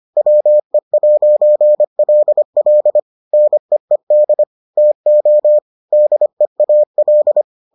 Morse code, heard as a keyed tone, about 25 wpm, 600 Hz. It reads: WE'LL NEED TO DEAL